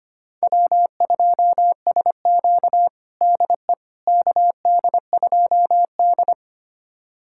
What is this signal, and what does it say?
Morse code, heard as a keyed tone, about 25 wpm, 700 Hz. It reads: W2SQ DE KD2D
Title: USA Callsign DE Callsign - 4 Characters - 25wpm